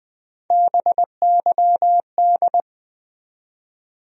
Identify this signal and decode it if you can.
Morse code, heard as a keyed tone, about 20 wpm, 700 Hz. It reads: BYD
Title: USA Call Sign Postfixes - 3 Letters 20wpm